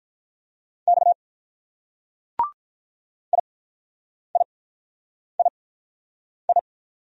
Morse code, heard as a keyed tone, about 44 wpm, spaced as for 9 wpm, 700 Hz. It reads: X IIII